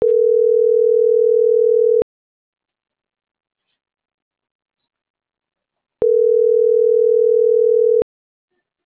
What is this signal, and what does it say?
Ringback tone (North America), 440 + 480 Hz.